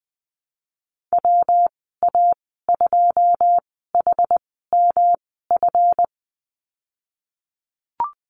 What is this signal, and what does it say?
Morse code, heard as a keyed tone, about 20 wpm, spaced as for 15 wpm, 700 Hz. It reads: WA2HMF
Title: 154 Lesson 25g - Letter p - Call Signs 20wpm